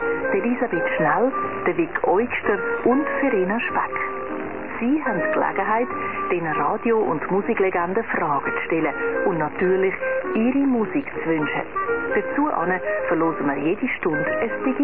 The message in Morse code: none